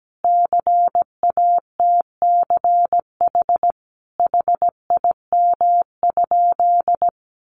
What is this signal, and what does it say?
Morse code, heard as a keyed tone, about 17 wpm, 700 Hz. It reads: CATCH HIM?
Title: Sentences from Top 1500 Words 17wpm